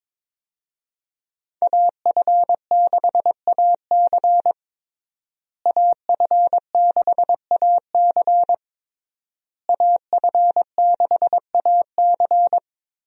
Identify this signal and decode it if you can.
Morse code, heard as a keyed tone, about 22 wpm, 700 Hz. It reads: AF6AC AF6AC AF6AC
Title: USA Call Signs - 5 Characters - Repeated 3x - 22wpm